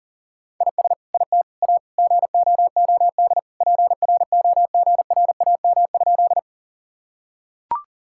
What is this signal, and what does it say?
Morse code, heard as a keyed tone, about 40 wpm, 700 Hz. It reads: IS IT A GOOD PROGRAM?
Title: Sentences from Top 1000 Words - Review Part 5 - 40wpm